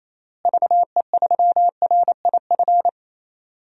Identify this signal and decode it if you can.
Morse code, heard as a keyed tone, about 28 wpm, 700 Hz. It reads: VE3RIF
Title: DXCC-100 Call Signs - 6 Characters - Repeated 3x - 28wpm